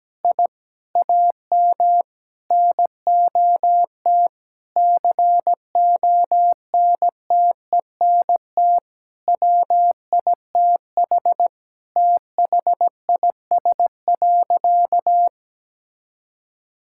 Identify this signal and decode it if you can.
Morse code, heard as a keyed tone, about 17 wpm, 700 Hz. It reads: I AM NOT CONTENT WITH THIS.
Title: Sentences from Top 1000 Words - Review Part 6 - 17wpm